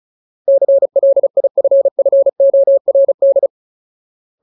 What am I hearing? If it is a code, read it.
Morse code, heard as a keyed tone, about 35 wpm, 550 Hz. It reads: CLIFFORD
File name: Ultimate Word List / 35WPM Ultimate